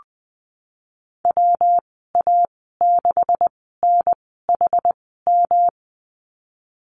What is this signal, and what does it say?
Morse code, heard as a keyed tone, about 19 wpm, spaced as for 14 wpm, 700 Hz. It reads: WA6NHM
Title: 154 Lesson 25g - Letter p - Call Signs 20wpm